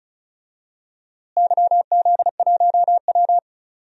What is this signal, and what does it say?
Morse code, heard as a keyed tone, about 34 wpm, 700 Hz. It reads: YZ1W
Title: DXCC-100 Call Signs - 4 Characters - Repeated 3x - 35wpm